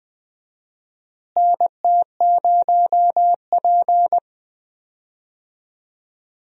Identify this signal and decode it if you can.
Morse code, heard as a keyed tone, about 20 wpm, 700 Hz. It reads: NT0P
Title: USA Call Signs - 4 Characters - Repeated 3x - 20wpm